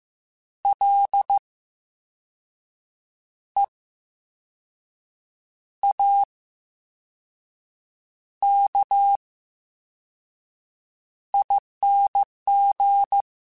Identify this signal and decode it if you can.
Morse code, heard as a keyed tone, about 15 wpm, spaced as for 5 wpm, 800 Hz. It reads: LEAKING
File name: Special INGs-mp3s / Special INGs-mp3s / VerySlowToFaster Ing @13